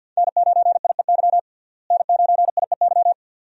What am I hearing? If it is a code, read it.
Morse code, heard as a keyed tone, about 50 wpm, 700 Hz. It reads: N9IEY N9IEY
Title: USA Call Signs - 5 Characters - Repeated 2x - 50wpm